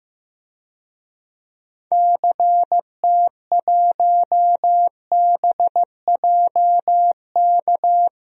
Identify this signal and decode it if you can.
Morse code, heard as a keyed tone, about 15 wpm, 700 Hz. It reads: CT1BJK
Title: DXCC-100 Call Signs - 6 Characters - Repeated 2x - 15wpm